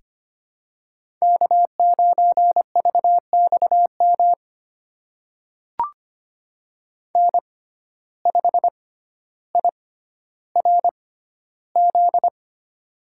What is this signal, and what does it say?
Morse code, heard as a keyed tone, about 25 wpm, spaced as for 13 wpm, 700 Hz. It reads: K9VXM N5IRZ
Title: USA Call Signs - 5 Characters - 6x Character Spacing - 25wpm